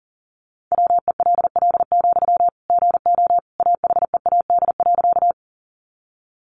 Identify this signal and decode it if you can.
Morse code, heard as a keyed tone, about 40 wpm, 700 Hz. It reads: WELL, GO AHEAD.
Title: Sentences from Top 1100 Words 40wpm